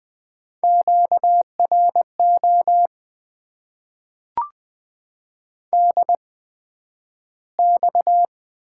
Morse code, heard as a keyed tone, about 20 wpm, spaced as for 11 wpm, 700 Hz. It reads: QRO DX
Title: Top 90 QSO Elements - 8x Character Spacing - 20wpm